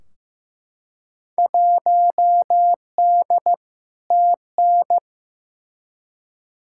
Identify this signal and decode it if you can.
Morse code, heard as a keyed tone, about 15 wpm, 700 Hz. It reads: 1D TN
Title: Field Day Exchanges - 15wpm